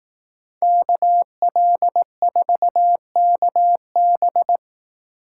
Morse code, heard as a keyed tone, about 18 wpm, 700 Hz. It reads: KL4KB
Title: Speed Racing - USA 2x2 Call Signs 15wpm